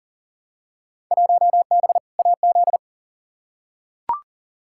Morse code, heard as a keyed tone, about 40 wpm, 700 Hz. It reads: 1B AZ
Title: Field Day Exchanges - 40wpm